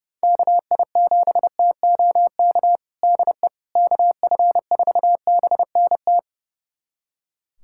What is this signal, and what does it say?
Morse code, heard as a keyed tone, about 32 wpm, 700 Hz. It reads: KI7TOK DE KF4BNT
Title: USA Callsign DE Callsign - 6 Characters - 30wpm